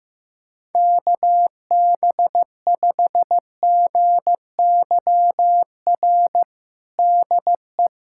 Morse code, heard as a keyed tone, about 15 wpm, 700 Hz. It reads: KB5GYR DE
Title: USA Callsign DE Callsign 15wpm